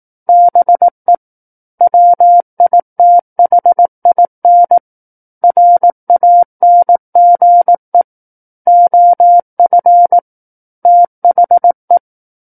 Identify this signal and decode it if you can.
Morse code, heard as a keyed tone, about 18 wpm, 700 Hz. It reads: BE WITHIN RANGE OF THE
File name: 18 WPM